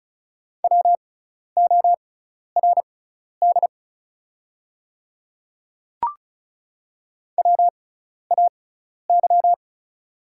Morse code, heard as a keyed tone, about 34 wpm, spaced as for 12 wpm, 700 Hz. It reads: WORD WAY
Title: 6x Character Spacing - Top 200 Words - 35wpm